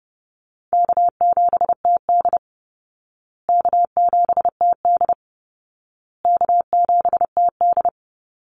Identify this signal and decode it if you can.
Morse code, heard as a keyed tone, about 30 wpm, 700 Hz. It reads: K7TD K7TD K7TD
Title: Easy Call Signs Repeated 3x - 30wpm